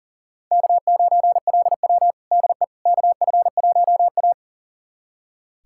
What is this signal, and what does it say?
Morse code, heard as a keyed tone, about 40 wpm, 700 Hz. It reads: K9LW DE KF1A
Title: USA Callsign DE Callsign - 4 Characters - 40wpm